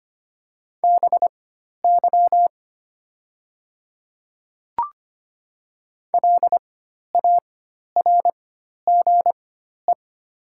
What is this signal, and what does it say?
Morse code, heard as a keyed tone, about 25 wpm, spaced as for 12 wpm, 700 Hz. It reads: BY LARGE
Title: 4x Character Spacing - Top 200 Words - 25wpm